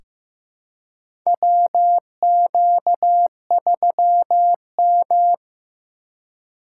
Morse code, heard as a keyed tone, about 15 wpm, 690 Hz. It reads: WQ3M
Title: USA Call Signs - 4 Characters - 2x Character Spacing - 15wpm